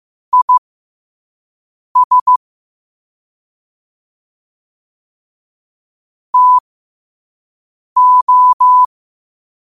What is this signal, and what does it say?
Morse code, heard as a keyed tone, about 15 wpm, spaced as for 5 wpm, 1000 Hz. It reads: IS TO
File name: qotdfiles / 5wpm